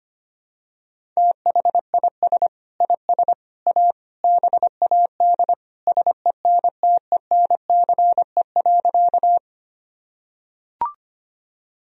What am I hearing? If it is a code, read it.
Morse code, heard as a keyed tone, about 25 wpm, 700 Hz. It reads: THIS IS A BAD SENTENCE.
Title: Sentences from Top 1000 Words - Review Part 1 - 25wpm